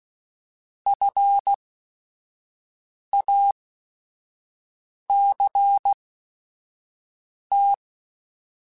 Morse code, heard as a keyed tone, about 16 wpm, spaced as for 5 wpm, 800 Hz. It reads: FACT
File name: Technical Subjects / Technical Subjects-mp3s / AND Gate @5